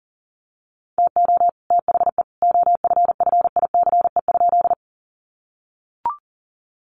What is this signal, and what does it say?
Morse code, heard as a keyed tone, about 40 wpm, 700 Hz. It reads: TO THE OFFICE?
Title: Sentences from Top 1000 Words 40wpm